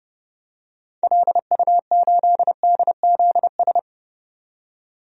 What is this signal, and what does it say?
Morse code, heard as a keyed tone, about 30 wpm, 700 Hz. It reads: LU8DZS
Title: DXCC-100 Call Signs - 6 Characters - 30wpm